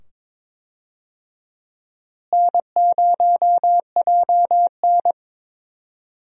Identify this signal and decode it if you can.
Morse code, heard as a keyed tone, about 22 wpm, 700 Hz. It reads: N0JN
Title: USA Call Signs - 4 Characters - Repeated 2x - 22wpm